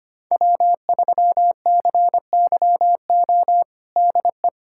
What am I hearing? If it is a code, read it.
Morse code, heard as a keyed tone, about 25 wpm, 700 Hz. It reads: W3CYO DE
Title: USA Callsign DE Callsign - 5 Characters - 25wpm